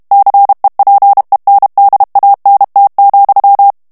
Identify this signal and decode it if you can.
Morse code, heard as a keyed tone, about 32 wpm, 800 Hz. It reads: CEPENDANT,
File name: morse mp3 / FRANCAIS TEXTE CLAIR 1 VITESSE 30 MOTS MINUTES